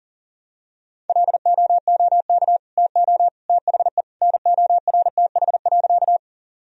Morse code, heard as a keyed tone, about 40 wpm, 700 Hz. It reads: LOOK TO THE NORTH.
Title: Sentences from Top 1000 Words - Review Part 1 - 40wpm